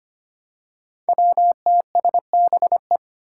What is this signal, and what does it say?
Morse code, heard as a keyed tone, about 25 wpm, 700 Hz. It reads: WTSBE
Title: Five Challenging Letter-Number Combinations 25wpm